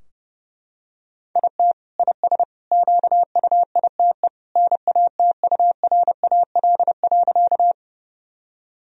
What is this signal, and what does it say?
Morse code, heard as a keyed tone, about 32 wpm, 700 Hz. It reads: IT IS QUITE NATURAL.
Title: Sentences from Top 1000 Words - Review Part 4 - 30wpm